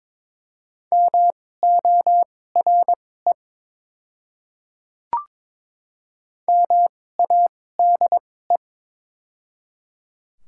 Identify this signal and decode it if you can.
Morse code, heard as a keyed tone, about 23 wpm, spaced as for 16 wpm, 700 Hz. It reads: MORE MADE
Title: 2x Character Spacing - Top 200 Words - 22wpm